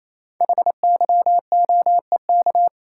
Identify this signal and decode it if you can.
Morse code, heard as a keyed tone, about 28 wpm, 700 Hz. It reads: HYOEK
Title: Code Groups - 5 Letters - 28wpm